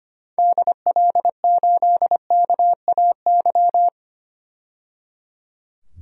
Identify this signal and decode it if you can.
Morse code, heard as a keyed tone, about 25 wpm, 700 Hz. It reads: DL8KAY
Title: DXCC-100 Call Signs - 6 Characters - 25wpm